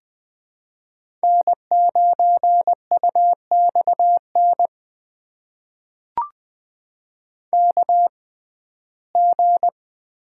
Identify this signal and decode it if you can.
Morse code, heard as a keyed tone, about 20 wpm, spaced as for 14 wpm, 700 Hz. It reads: N9UXN KG
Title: USA Call Signs - 6x Character Spacing - 20wpm